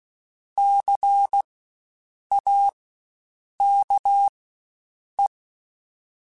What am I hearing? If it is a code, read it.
Morse code, heard as a keyed tone, about 16 wpm, spaced as for 7 wpm, 800 Hz. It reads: CAKE
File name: Spaced Alphas / Spaced Alphas mp3s / Cs sp @15